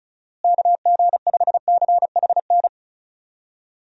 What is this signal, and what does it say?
Morse code, heard as a keyed tone, about 34 wpm, 700 Hz. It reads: KG5CHN
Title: USA Call Signs - 6 Characters - Repeated 3x - 35wpm